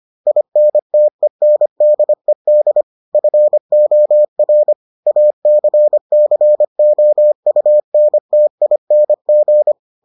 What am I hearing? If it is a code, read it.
Morse code, heard as a keyed tone, about 25 wpm, 600 Hz. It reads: INTENDED FOR ACCOUNTING